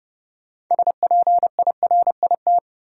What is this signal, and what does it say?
Morse code, heard as a keyed tone, about 32 wpm, 700 Hz. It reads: SPIRIT
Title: Sentences from Top 1400 Words - New Word Drill - 30wpm